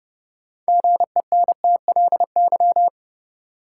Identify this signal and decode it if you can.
Morse code, heard as a keyed tone, about 32 wpm, 700 Hz. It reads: GENTLY
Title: Sentences from Top 1500 Words - New Word Drill Alt - 30wpm